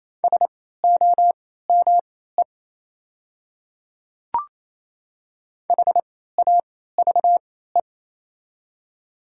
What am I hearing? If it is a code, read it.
Morse code, heard as a keyed tone, about 28 wpm, spaced as for 16 wpm, 690 Hz. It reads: SOME HAVE
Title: 3x Character Spacing - Top 200 Words - 28wpm